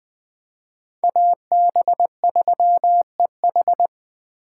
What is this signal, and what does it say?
Morse code, heard as a keyed tone, about 20 wpm, 700 Hz. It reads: AB3EH